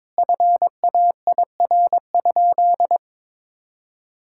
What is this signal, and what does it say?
Morse code, heard as a keyed tone, about 22 wpm, 700 Hz. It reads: FAIR?
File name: practice / 22wpm / Sentences with 4-Word Phrases - 22wpm